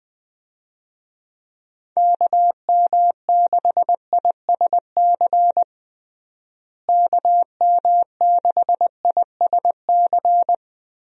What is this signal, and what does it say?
Morse code, heard as a keyed tone, about 20 wpm, 700 Hz. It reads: KM6ISC KM6ISC
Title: USA Call Signs - 6 Characters - Repeated 3x - 20wpm